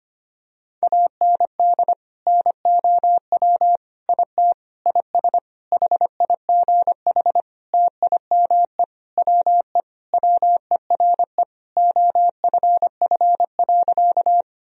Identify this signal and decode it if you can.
Morse code, heard as a keyed tone, about 25 wpm, 700 Hz. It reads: AND NOW IT IS HIGH TIME WE WERE OFF.
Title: Sentences from Top 1000 Words - Review Part 1 - 25wpm